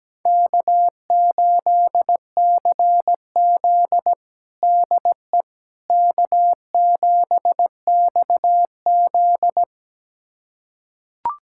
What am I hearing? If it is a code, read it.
Morse code, heard as a keyed tone, about 17 wpm, 700 Hz. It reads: K8CZ DE K7XZ
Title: USA Callsign DE Callsign - 4 Characters - 17wpm